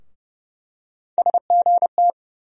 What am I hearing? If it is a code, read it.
Morse code, heard as a keyed tone, about 30 wpm, 700 Hz. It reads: SGT
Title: USA Call Sign Postfixes - 3 Letters 30wpm